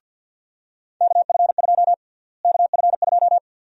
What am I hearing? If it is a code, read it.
Morse code, heard as a keyed tone, about 48 wpm, 700 Hz. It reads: KF2 KF2
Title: USA Call Sign Prefixes - Repeated 2x - 50wpm